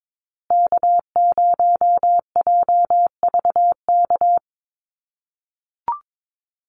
Morse code, heard as a keyed tone, about 22 wpm, 690 Hz. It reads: K0JVK
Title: USA Call Signs Repeated 3x 22wpm